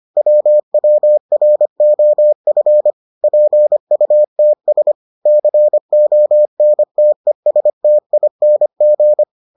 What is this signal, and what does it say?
Morse code, heard as a keyed tone, about 25 wpm, 600 Hz. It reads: WWROF PUTS CONTESTING